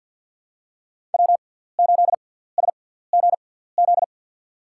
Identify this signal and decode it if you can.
Morse code, heard as a keyed tone, about 53 wpm, spaced as for 17 wpm, 700 Hz. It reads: W8SGZ
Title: USA Call Signs - 5 Characters - 6x Character Spacing - 50wpm